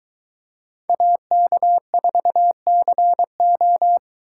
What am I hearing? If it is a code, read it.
Morse code, heard as a keyed tone, about 23 wpm, 700 Hz. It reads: AK4CO